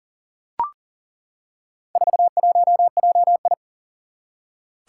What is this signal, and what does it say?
Morse code, heard as a keyed tone, about 38 wpm, 700 Hz. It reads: 41JI
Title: Four Challenging Letter-Number Combinations 40wpm